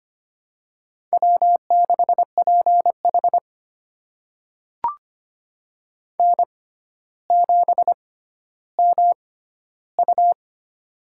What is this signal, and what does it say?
Morse code, heard as a keyed tone, about 25 wpm, spaced as for 13 wpm, 700 Hz. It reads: W6PH N7MU
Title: USA Call Signs - 4 Characters - 6x Character Spacing - 25wpm